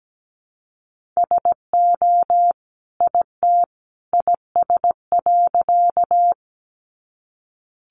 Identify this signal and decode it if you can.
Morse code, heard as a keyed tone, about 17 wpm, 700 Hz. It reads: SO IT IS.
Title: Sentences from Top 100 Words 17wpm